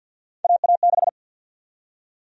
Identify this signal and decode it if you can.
Morse code, heard as a keyed tone, about 52 wpm, 700 Hz. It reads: AA6
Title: USA Call Sign Prefixes - 50wpm